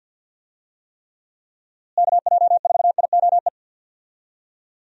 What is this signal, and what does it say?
Morse code, heard as a keyed tone, about 51 wpm, 700 Hz. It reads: KJ4IOE
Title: USA Call Signs - 6 Characters - Repeated 2x - 50wpm